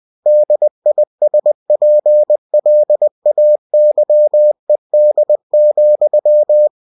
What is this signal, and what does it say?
Morse code, heard as a keyed tone, about 20 wpm, 600 Hz. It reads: DISPLAYED,
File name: winnie-20wpm